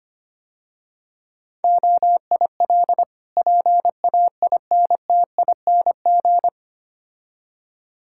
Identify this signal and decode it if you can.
Morse code, heard as a keyed tone, about 26 wpm, 700 Hz. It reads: OIL PAINTING
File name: practice / 25wpm / Difficult Words - Speed-Racing with Hints - 25wpm